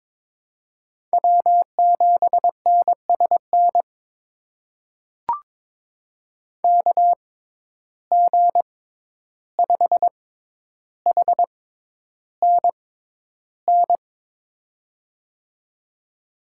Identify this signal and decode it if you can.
Morse code, heard as a keyed tone, about 22 wpm, spaced as for 11 wpm, 700 Hz. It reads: W7NSN KG5HNN
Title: USA Call Signs - 6x Character Spacing - 22wpm